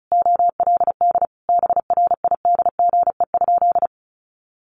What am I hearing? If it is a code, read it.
Morse code, heard as a keyed tone, about 34 wpm, 700 Hz. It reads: OLD BRIDGE?
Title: Sentences from Top 1200 Words 35wpm